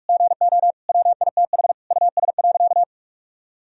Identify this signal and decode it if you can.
Morse code, heard as a keyed tone, about 46 wpm, 700 Hz. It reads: GO WITH US.
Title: Sentences from Top 1000 Words - Review Part 3 - 45wpm